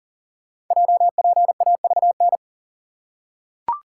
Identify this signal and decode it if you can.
Morse code, heard as a keyed tone, about 40 wpm, 700 Hz. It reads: JPAVN
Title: Code Groups - 5 Letters - 40wpm